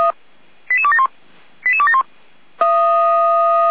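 Five-tone selective call (ZVEI), heard as quick runs of five tones, tones of about 70 ms. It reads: 88271 88271
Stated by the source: ZVEI five-tone selective call received on an SDR.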